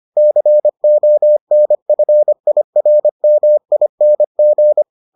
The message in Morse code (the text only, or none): CONFIRMING